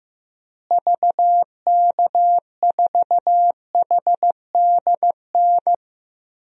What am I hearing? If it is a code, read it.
Morse code, heard as a keyed tone, about 15 wpm, 700 Hz. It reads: VK4HDN
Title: DXCC-100 Call Signs - 6 Characters - Repeated 3x - 15wpm